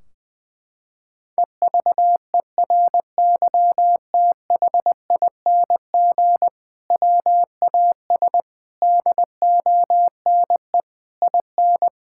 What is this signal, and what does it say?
Morse code, heard as a keyed tone, about 21 wpm, 700 Hz. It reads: EVERYTHING WAS DONE IN
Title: Sentences from Top 1600 Words 20wpm